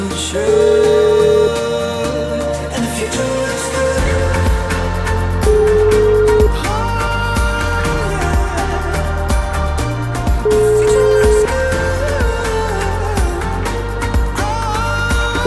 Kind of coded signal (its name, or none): telephone call progress tone